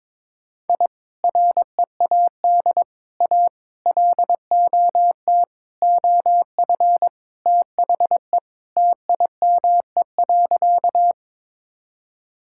Morse code, heard as a keyed tone, about 22 wpm, 700 Hz. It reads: I READ A LOT OF THE TIME.
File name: practice / 22wpm / Sentences with 4-Word Phrases - 22wpm